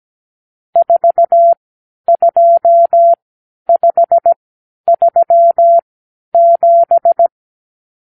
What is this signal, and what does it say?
Morse code, heard as a keyed tone, about 17 wpm, spaced as for 10 wpm, 680 Hz. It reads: 42537
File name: k7qo / Lesson 52 (8)